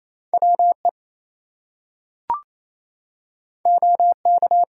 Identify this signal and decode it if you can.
Morse code, heard as a keyed tone, about 29 wpm, 700 Hz. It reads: WE OK
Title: Two Letter Words - 28wpm